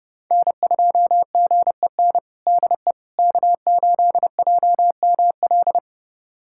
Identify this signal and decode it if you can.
Morse code, heard as a keyed tone, about 30 wpm, 700 Hz. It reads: N2GEN DE K8JML
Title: USA Callsign DE Callsign - 5 Characters - 30wpm